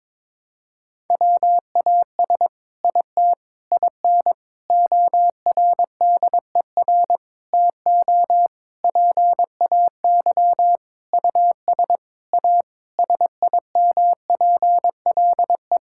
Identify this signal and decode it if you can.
Morse code, heard as a keyed tone, about 22 wpm, 700 Hz. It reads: WAS IT IN ORDER TO PAY US A SIMPLE
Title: Sentences from Top 1000 Words - Review Part 6 - 22wpm